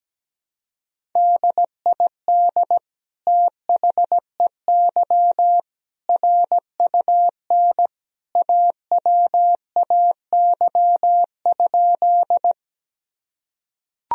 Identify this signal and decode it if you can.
Morse code, heard as a keyed tone, about 17 wpm, 700 Hz. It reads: DID THEY RUN AWAY?
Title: Sentences from Top 1000 Words - Review Part 1 - 17wpm